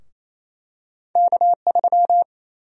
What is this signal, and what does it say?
Morse code, heard as a keyed tone, about 28 wpm, 700 Hz. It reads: K3
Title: USA Call Sign Prefixes - 28wpm